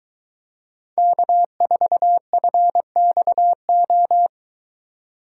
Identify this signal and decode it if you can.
Morse code, heard as a keyed tone, about 23 wpm, 700 Hz. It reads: K4FXO